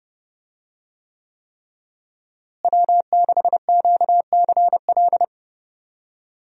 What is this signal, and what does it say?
Morse code, heard as a keyed tone, about 28 wpm, 700 Hz. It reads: W6QCL
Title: USA Call Signs - 5 Characters - Repeated 3x - 30wpm